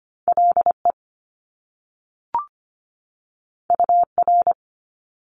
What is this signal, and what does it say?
Morse code, heard as a keyed tone, about 25 wpm, 700 Hz. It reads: LE UR
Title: Top Two Letter Combinations in Words 25wpm